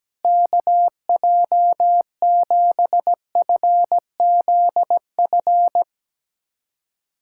Morse code, heard as a keyed tone, about 17 wpm, 700 Hz. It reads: KJ7FZF
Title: USA Callsign DE Callsign 17wpm